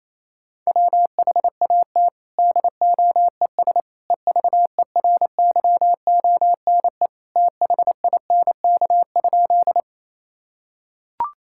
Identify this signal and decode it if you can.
Morse code, heard as a keyed tone, about 28 wpm, 700 Hz. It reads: WHAT DOES EVERYONE THINK?
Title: Sentences from Top 1000 Words - Review Part 5 - 28wpm